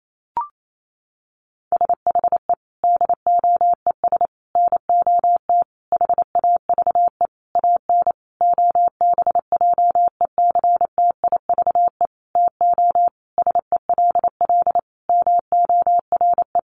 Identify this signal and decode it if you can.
Morse code, heard as a keyed tone, about 28 wpm, 700 Hz. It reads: SHE DOES NOT HAVE AN OBJECTIVE TO SELL MORE
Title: Sentences from Top 1200 Words 28wpm